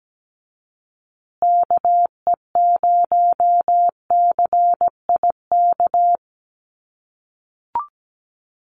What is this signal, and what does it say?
Morse code, heard as a keyed tone, about 17 wpm, 700 Hz. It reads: KE0CIK
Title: USA Call Signs Repeated 3x 17wpm